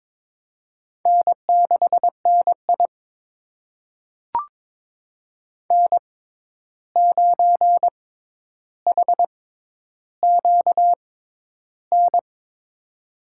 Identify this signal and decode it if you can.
Morse code, heard as a keyed tone, about 21 wpm, spaced as for 10 wpm, 700 Hz. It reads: N6NI N9HQN
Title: USA Call Signs - 6x Character Spacing - 22wpm